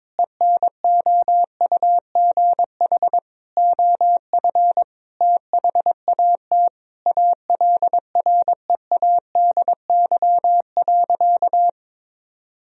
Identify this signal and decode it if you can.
Morse code, heard as a keyed tone, about 22 wpm, 700 Hz. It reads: ENOUGH OF THAT ALREADY.
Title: Sentences from Top 1000 Words - Review Part 3 - 22wpm